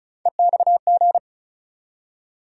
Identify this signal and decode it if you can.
Morse code, heard as a keyed tone, about 34 wpm, 700 Hz. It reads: EXG
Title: USA Call Sign Postfixes - 3 Letters 35wpm